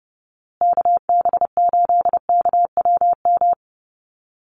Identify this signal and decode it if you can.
Morse code, heard as a keyed tone, about 29 wpm, 700 Hz. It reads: KB8KWM
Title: USA Call Signs Repeated 3x 30wpm